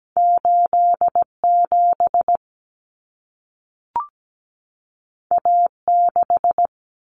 Morse code, heard as a keyed tone, about 17 wpm, 700 Hz. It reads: 87 A6
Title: Common Cut Numbers - 2-Digit 17wpm